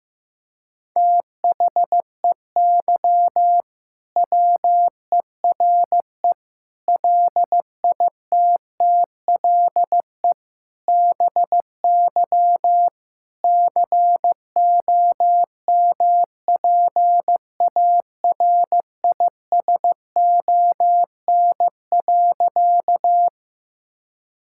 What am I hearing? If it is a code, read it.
Morse code, heard as a keyed tone, about 15 wpm, 700 Hz. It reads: THEY WERE LITTLE BY COMPARISON.
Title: Sentences from Top 1600 Words 15wpm